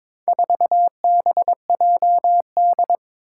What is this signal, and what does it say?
Morse code, heard as a keyed tone, about 22 wpm, 700 Hz. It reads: 4BJD
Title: Four Challenging Letter-Number Combinations 22wpm